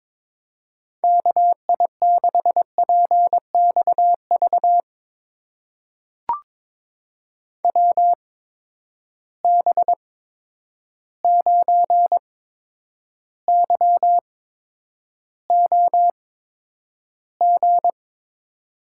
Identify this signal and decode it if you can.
Morse code, heard as a keyed tone, about 22 wpm, spaced as for 9 wpm, 700 Hz. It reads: KI6PXV WB9YOG
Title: USA Call Signs - 6 Characters - 8x Character Spacing - 22wpm